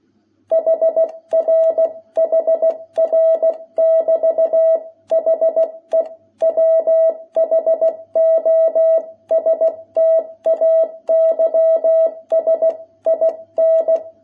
Morse code, heard as a keyed tone, about 16 wpm, 660 Hz. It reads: HR HR<BT>HE WHO STAYS IN
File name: jh8cbh / 2021cw